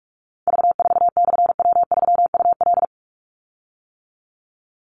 Morse code, heard as a keyed tone, about 44 wpm, 700 Hz. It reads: V4/W3UL
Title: Call Signs - 45wpm